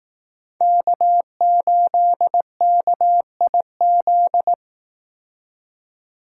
Morse code, heard as a keyed tone, about 18 wpm, 700 Hz. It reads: K8KIZ
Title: Speed Racing - USA Call Signs 15wpm